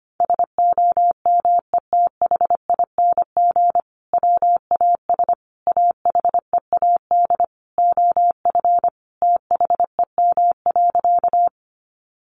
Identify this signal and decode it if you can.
Morse code, heard as a keyed tone, about 25 wpm, 700 Hz. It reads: SOMETHING WAS AHEAD OF THEM.
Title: Sentences from Top 1100 Words 25wpm